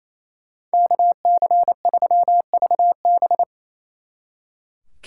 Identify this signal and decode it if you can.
Morse code, heard as a keyed tone, about 35 wpm, 700 Hz. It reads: KC3VB
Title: Speed Racing - USA 2x2 Call Signs 28wpm